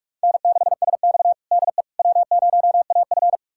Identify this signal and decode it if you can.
Morse code, heard as a keyed tone, about 43 wpm, 700 Hz. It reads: N6SX DE W0AF
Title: USA Callsign DE Callsign - 4 Characters - 45wpm